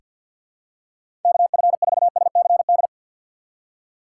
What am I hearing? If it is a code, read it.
Morse code, heard as a keyed tone, about 47 wpm, 700 Hz. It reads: KF4SCD
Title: USA Call Signs - 6 Characters - 4x Character Spacing - 50wpm